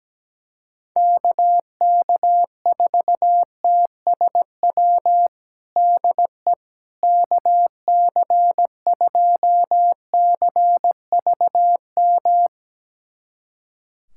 Morse code, heard as a keyed tone, about 17 wpm, 700 Hz. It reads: KK4TSW DE KC2CVM
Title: USA Callsign DE Callsign - 6 Characters - 17wpm